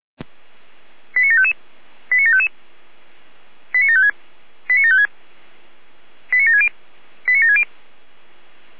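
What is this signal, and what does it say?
Five-tone selective call (ZVEI), heard as quick runs of five tones, tones of about 70 ms. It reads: 89755 89755 89756 89756 89760 89760
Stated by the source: ZVEI five-tone selective call received by SDR.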